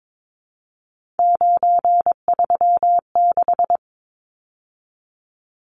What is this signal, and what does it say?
Morse code, heard as a keyed tone, about 22 wpm, 700 Hz. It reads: 936
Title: Numbers - 3-Digit 22wpm